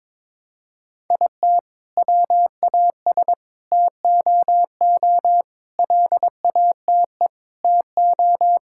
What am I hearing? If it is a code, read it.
Morse code, heard as a keyed tone, about 22 wpm, 700 Hz. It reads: IT WAS TOO LATE TO
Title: Sentences from Top 1600 Words 22wpm